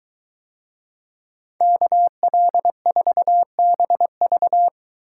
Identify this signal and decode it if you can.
Morse code, heard as a keyed tone, about 21 wpm, 700 Hz. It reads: KL4BV